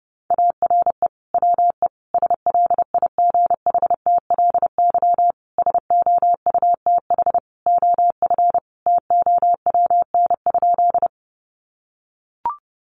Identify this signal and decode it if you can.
Morse code, heard as a keyed tone, about 30 wpm, 700 Hz. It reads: ARE WE SLIGHTLY SOUTH OF TOWN?
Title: Sentences from Top 1300 Words 30wpm